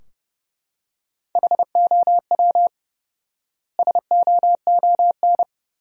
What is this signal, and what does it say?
Morse code, heard as a keyed tone, about 32 wpm, 700 Hz. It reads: HOW SOON
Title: 4x Word-Spacing - Sentences from Top 400 Words - 30wpm